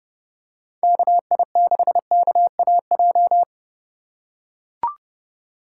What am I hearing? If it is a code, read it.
Morse code, heard as a keyed tone, about 30 wpm, 700 Hz. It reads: KI6KAJ
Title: USA Call Signs - 6 Characters - 4x Character Spacing - 30wpm